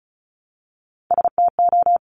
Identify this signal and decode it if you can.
Morse code, heard as a keyed tone, about 36 wpm, 700 Hz. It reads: STO
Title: Top Three Letter Combinations in Words 35wpm